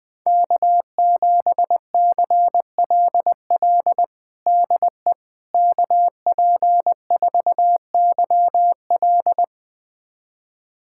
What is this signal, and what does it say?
Morse code, heard as a keyed tone, about 20 wpm, 700 Hz. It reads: K7CLL DE KP4YL
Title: USA Callsign DE Callsign - 5 Characters - 20wpm